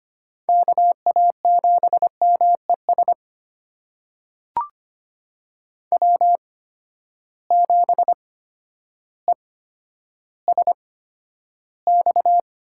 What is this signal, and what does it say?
Morse code, heard as a keyed tone, about 25 wpm, spaced as for 11 wpm, 700 Hz. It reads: KA7MES W7ESX
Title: USA Call Signs - 8x Character Spacing - 25wpm